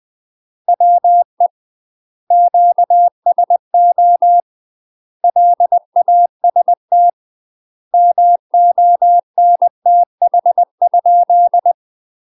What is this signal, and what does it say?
Morse code, heard as a keyed tone, about 20 wpm, 700 Hz. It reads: WE QSO LAST MONTH?
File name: UK MASTER QSOs TRAINING / 20WPM / UK MASTER QSOs 0003